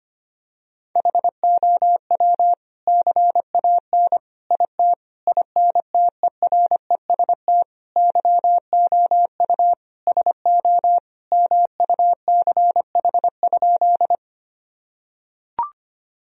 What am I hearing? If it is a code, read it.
Morse code, heard as a keyed tone, about 25 wpm, 700 Hz. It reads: HOW CAN IT INTEREST YOU SO MUCH?
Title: Sentences from Top 1000 Words - Review Part 2 - 25wpm